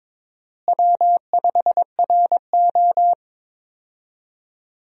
Morse code, heard as a keyed tone, about 22 wpm, 700 Hz. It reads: W5RO
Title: USA Call Signs - 4 Characters - Repeated 2x - 22wpm